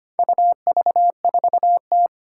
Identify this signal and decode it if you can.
Morse code, heard as a keyed tone, about 25 wpm, 700 Hz. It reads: UV4T
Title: Four Challenging Letter-Number Combinations 25wpm